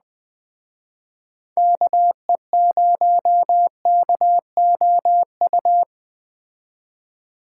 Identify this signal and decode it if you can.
Morse code, heard as a keyed tone, about 20 wpm, 700 Hz. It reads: KE0KOU